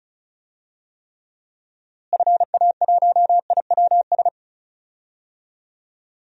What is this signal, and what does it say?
Morse code, heard as a keyed tone, about 34 wpm, 700 Hz. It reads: FA1IWS
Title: DXCC-100 Call Signs - 6 Characters - Repeated 3x - 35wpm